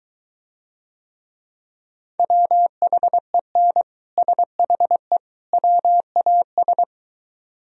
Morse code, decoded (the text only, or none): WHEN SHE WAS